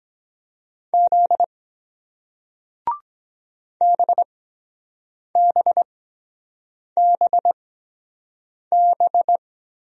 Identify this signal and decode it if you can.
Morse code, heard as a keyed tone, about 23 wpm, spaced as for 7 wpm, 700 Hz. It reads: Z BBBB